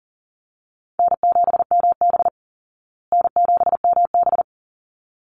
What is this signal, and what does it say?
Morse code, heard as a keyed tone, about 40 wpm, 700 Hz. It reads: N7MB N7MB
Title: Easy Call Signs Repeated 2x - 40wpm